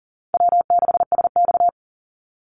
Morse code, heard as a keyed tone, about 38 wpm, 700 Hz. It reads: W6SX
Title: Easy Call Signs - 40wpm